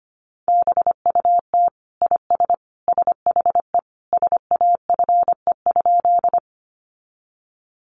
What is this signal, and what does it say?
Morse code, heard as a keyed tone, about 25 wpm, 700 Hz. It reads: BUT IS SHE SAFE?
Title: Sentences from Top 900 Words 25wpm